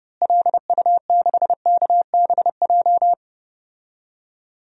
Morse code, heard as a keyed tone, about 30 wpm, 700 Hz. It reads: LU6KBJ
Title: DXCC-100 Call Signs - 6 Characters - Repeated 3x - 30wpm